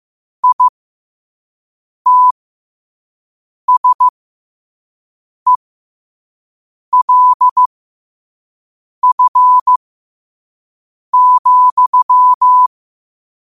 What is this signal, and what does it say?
Morse code, heard as a keyed tone, about 15 wpm, spaced as for 5 wpm, 1000 Hz. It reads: ITSELF,
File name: qotdfiles / 5wpm